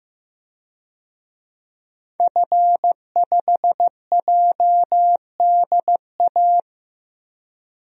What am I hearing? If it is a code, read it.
Morse code, heard as a keyed tone, about 15 wpm, 700 Hz. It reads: F5JDA
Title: DXCC-100 Call Signs - 5 Characters - Repeated 3x - 15wpm